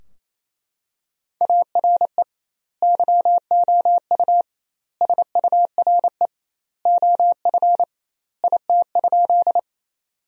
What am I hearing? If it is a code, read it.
Morse code, heard as a keyed tone, about 27 wpm, 700 Hz. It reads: ARE YOU SURE OF IT?
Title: 2x Word-Spacing - Sentences from Top 300 Words - 28wpm